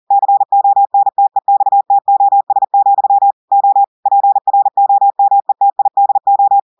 Morse code, heard as a keyed tone, about 40 wpm, 800 Hz. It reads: CONTEXTOS, O PROMETIDO
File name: audios / portugais / portugais texte en clair 1 vitesse 40 mots minutes